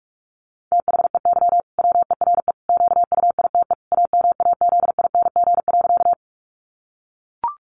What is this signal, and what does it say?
Morse code, heard as a keyed tone, about 44 wpm, 700 Hz. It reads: THEY WERE QUITE AMAZING.
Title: Sentences from Top 1500 Words 45wpm